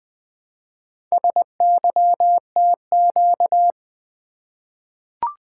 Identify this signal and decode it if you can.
Morse code, heard as a keyed tone, about 20 wpm, 700 Hz. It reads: SYTQ
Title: Four Letter-Number Combinations - 20wpm